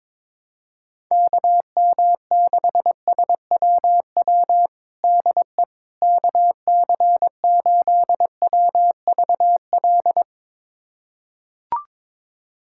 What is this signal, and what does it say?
Morse code, heard as a keyed tone, about 22 wpm, 700 Hz. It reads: KM6SWW DE KC8WVL
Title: USA Callsign DE Callsign 22wpm